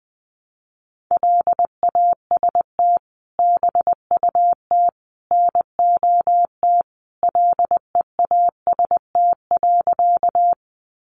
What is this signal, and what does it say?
Morse code, heard as a keyed tone, about 20 wpm, 700 Hz. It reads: LAST BUT NOT LEAST.
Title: Sentences from Top 700 Words 20wpm